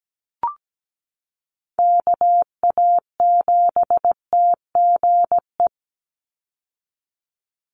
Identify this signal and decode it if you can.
Morse code, heard as a keyed tone, about 17 wpm, 700 Hz. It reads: KA7TGE
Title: USA Call Signs 17wpm